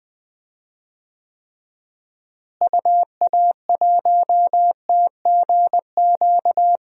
Morse code, heard as a keyed tone, about 21 wpm, 700 Hz. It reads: UA1TGQ
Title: DXCC-100 Call Signs - 6 Characters - Repeated 2x - 20wpm